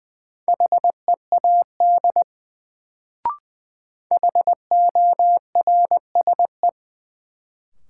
Morse code, heard as a keyed tone, about 20 wpm, 700 Hz. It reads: HEAD HORSE
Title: Sentences from Top 300 Words - New Word Drill - 20wpm